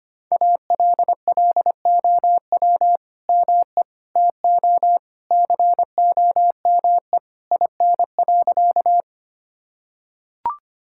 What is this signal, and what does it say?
Morse code, heard as a keyed tone, about 25 wpm, 700 Hz. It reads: ALLOW ME TO COME IN.
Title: Sentences from Top 1000 Words - Review Part 5 - 25wpm